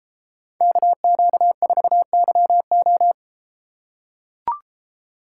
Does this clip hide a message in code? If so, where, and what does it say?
Yes: Morse code from 0.6 to 3.1 s, heard as a keyed tone, about 33 wpm, 700 Hz. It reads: KQ4YO